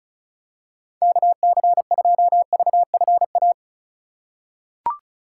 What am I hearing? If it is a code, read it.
Morse code, heard as a keyed tone, about 34 wpm, 690 Hz. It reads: KC2VFA
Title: USA Call Signs - 6 Characters - 4x Character Spacing - 35wpm